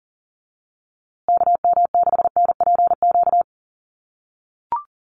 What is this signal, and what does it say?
Morse code, heard as a keyed tone, about 40 wpm, 700 Hz. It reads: KM6NPQ
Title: USA Call Signs Repeated 3x 40wpm